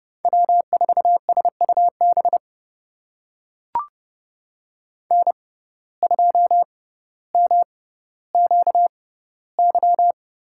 Morse code, heard as a keyed tone, about 30 wpm, spaced as for 15 wpm, 700 Hz. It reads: W4SUB N2MQY
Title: USA Call Signs - 5 Characters - 6x Character Spacing - 30wpm